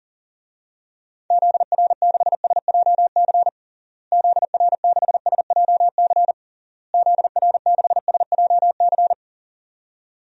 Morse code, heard as a keyed tone, about 40 wpm, 700 Hz. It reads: ZR6SJC ZR6SJC ZR6SJC
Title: DXCC-100 Call Signs - 6 Characters - Repeated 3x - 40wpm